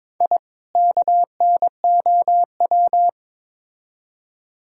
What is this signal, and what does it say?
Morse code, heard as a keyed tone, about 22 wpm, 700 Hz. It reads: I KNOW
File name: Build A Phrase - 2-Word Phrases - 22wpm